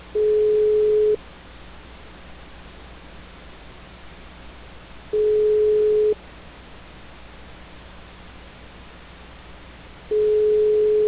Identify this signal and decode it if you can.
Ringback tone (Europe, 425 Hz).
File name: out- -2011- - - .19594